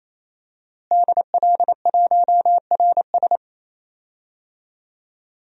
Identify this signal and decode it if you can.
Morse code, heard as a keyed tone, about 28 wpm, 700 Hz. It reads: DL1RS
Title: DXCC-100 Call Signs - 5 Characters - Repeated 3x - 28wpm